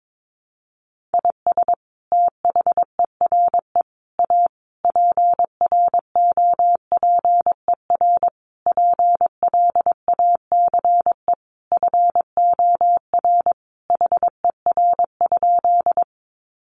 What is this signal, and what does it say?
Morse code, heard as a keyed tone, about 22 wpm, 700 Hz. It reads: IS THERE A PROPER PLACE FOR HER?
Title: Sentences from Top 800 Words 22wpm